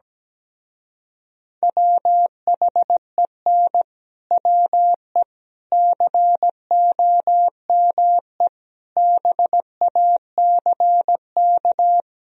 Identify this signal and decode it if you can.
Morse code, heard as a keyed tone, about 17 wpm, 700 Hz. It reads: WHEN WE COME BACK
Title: Speed-Racing - Sets of 4 Words 17wpm